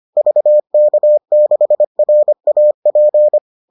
Morse code, heard as a keyed tone, about 25 wpm, 600 Hz. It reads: VK6RAP